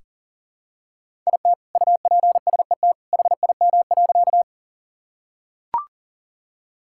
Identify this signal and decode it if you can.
Morse code, heard as a keyed tone, about 37 wpm, 700 Hz. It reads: IT UPSET HIM.
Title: Sentences from Top 1000 Words - Review Part 5 - 40wpm